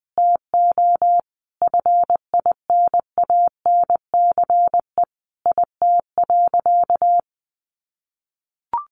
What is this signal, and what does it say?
Morse code, heard as a keyed tone, about 19 wpm, 700 Hz. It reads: TO FINANCE IT.
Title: Sentences from Top 1500 Words 20wpm